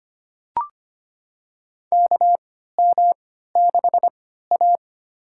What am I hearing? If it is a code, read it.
Morse code, heard as a keyed tone, about 25 wpm, spaced as for 14 wpm, 700 Hz. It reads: KM6A
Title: USA Call Signs - 4 Characters - 3x Character Spacing - 25wpm